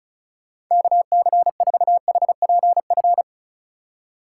Morse code, heard as a keyed tone, about 45 wpm, 700 Hz. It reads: KC4HPF